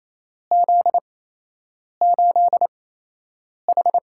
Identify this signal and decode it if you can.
Morse code, heard as a keyed tone, about 28 wpm, spaced as for 8 wpm, 700 Hz. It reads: Z8H
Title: USA Call Signs - 4 Characters - 8x Character Spacing - 28wpm